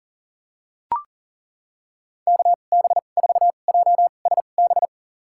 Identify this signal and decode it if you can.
Morse code, heard as a keyed tone, about 39 wpm, spaced as for 29 wpm, 700 Hz. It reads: KB4JSB
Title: USA Call Signs - 6 Characters - 2x Character Spacing - 40wpm